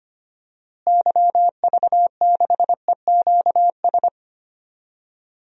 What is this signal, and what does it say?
Morse code, heard as a keyed tone, about 25 wpm, 700 Hz. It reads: YV6EQS
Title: DXCC-100 Call Signs - 6 Characters - Repeated 3x - 25wpm